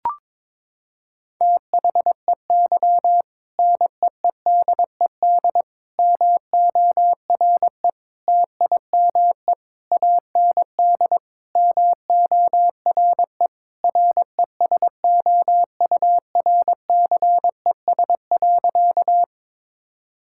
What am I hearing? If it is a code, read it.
Morse code, heard as a keyed tone, about 21 wpm, 700 Hz. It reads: THEY NEEDED MORE TIME AND MORE RESOURCES.
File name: practice / 22wpm / Sentences with 2-Word Phrases - 22wpm